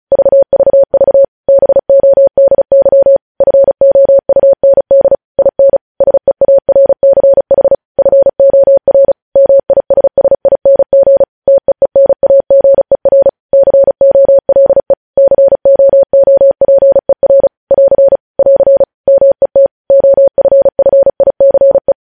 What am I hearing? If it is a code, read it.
Morse code, heard as a keyed tone, about 36 wpm, 550 Hz. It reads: VVV BODY FOUND IN SEARCH FOR MISSING TEENAGER COLE COOPER <AR> <AR> MET OFFICE
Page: News Headlines in Morse Code at 35 WPM